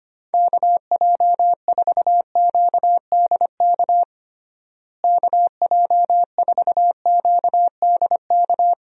Morse code, heard as a keyed tone, about 25 wpm, 700 Hz. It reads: KJ4QDK KJ4QDK
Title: USA Call Signs - 6 Characters - Repeated 3x - 25wpm